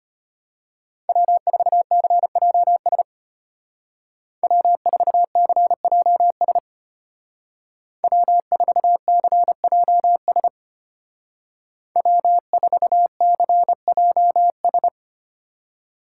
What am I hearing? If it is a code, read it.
Morse code, heard as a keyed tone, about 31 wpm, 700 Hz. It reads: W4CJS W4CJS W4CJS W4CJS
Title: Speed Racing - USA 1x3 Call Signs 25wpm